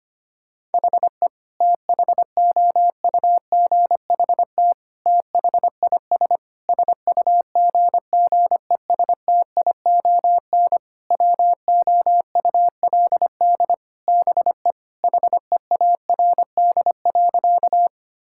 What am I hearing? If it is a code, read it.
Morse code, heard as a keyed tone, about 25 wpm, 700 Hz. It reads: HE THOUGHT THIS SUGGESTION WOULD BE HEARD.
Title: Sentences from Top 1600 Words 25wpm